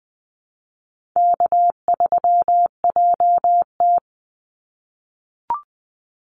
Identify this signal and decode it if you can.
Morse code, heard as a keyed tone, about 20 wpm, 700 Hz. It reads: K3JT
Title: Easy Call Signs - 20wpm